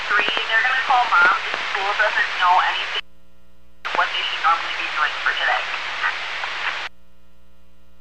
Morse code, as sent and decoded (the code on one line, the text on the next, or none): none